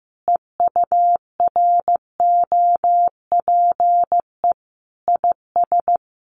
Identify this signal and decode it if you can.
Morse code, heard as a keyed tone, about 15 wpm, 700 Hz. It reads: EUROPE IS
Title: A Sound Of Thunder 15wpm